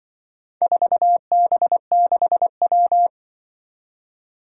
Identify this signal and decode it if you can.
Morse code, heard as a keyed tone, about 25 wpm, 700 Hz. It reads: 4B6W
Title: Speed-Racing - Four Letter-Number Combinations - 20wpm